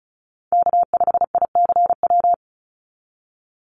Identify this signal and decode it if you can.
Morse code, heard as a keyed tone, about 34 wpm, 700 Hz. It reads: K5ICW
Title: USA Call Signs 35wpm